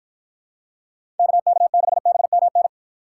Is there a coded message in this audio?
Yes: Morse code from 1.2 to 2.7 s, heard as a keyed tone, about 49 wpm, 700 Hz. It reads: KK6BMN